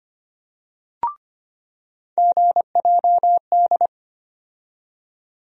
Morse code, heard as a keyed tone, about 25 wpm, 700 Hz. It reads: GJD
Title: USA Call Sign Postfixes - 3 Letters 25wpm